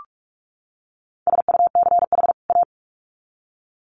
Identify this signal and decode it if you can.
Morse code, heard as a keyed tone, about 44 wpm, 700 Hz. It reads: SUCH A
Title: Sets of 2 Words 45wpm